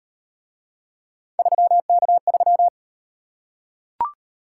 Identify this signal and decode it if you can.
Morse code, heard as a keyed tone, about 38 wpm, 700 Hz. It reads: 3K3